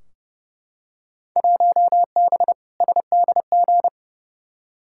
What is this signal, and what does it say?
Morse code, heard as a keyed tone, about 30 wpm, 700 Hz. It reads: 1B SDG
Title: Field Day Exchanges Encore - 30wpm